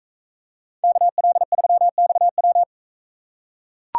Morse code, heard as a keyed tone, about 42 wpm, 700 Hz. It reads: KL3XW